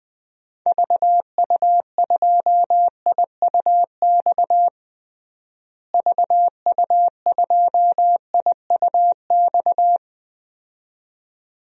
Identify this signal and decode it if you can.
Morse code, heard as a keyed tone, about 20 wpm, 700 Hz. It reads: VU2IUX VU2IUX
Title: DXCC-100 Call Signs - 6 Characters - Repeated 2x - 20wpm